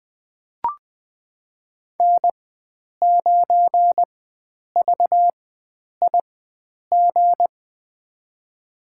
Morse code, heard as a keyed tone, about 20 wpm, spaced as for 9 wpm, 700 Hz. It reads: N9VIG
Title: USA Call Signs - 5 Characters - 4x Character Spacing - 20wpm